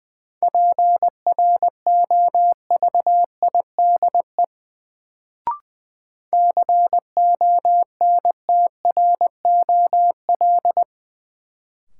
Morse code, heard as a keyed tone, about 20 wpm, 700 Hz. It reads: PROVIDE CONTROL
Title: Sentences from Top 1000 Words - Review Part 2 - New Word Drill - 20wpm